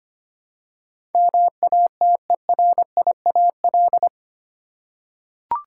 Morse code, heard as a keyed tone, about 25 wpm, 700 Hz. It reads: MATERIAL
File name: Eight Letter Words - 25wpm